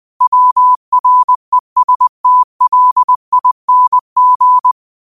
Morse code, heard as a keyed tone, about 20 wpm, 1000 Hz. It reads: WRESTLING